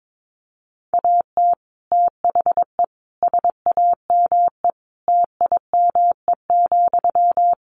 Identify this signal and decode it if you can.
Morse code, heard as a keyed tone, about 22 wpm, 700 Hz. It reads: AT THE SAME TIME,
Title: Sentences from Top 1500 Words 22wpm